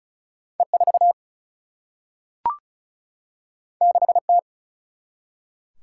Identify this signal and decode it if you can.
Morse code, heard as a keyed tone, about 36 wpm, 700 Hz. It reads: E 4 6 T
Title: Single-Letter Single-Number 35wpm